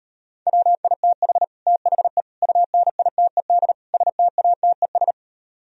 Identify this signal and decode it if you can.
Morse code, heard as a keyed tone, about 38 wpm, 700 Hz. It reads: WITH THE UNITED STATES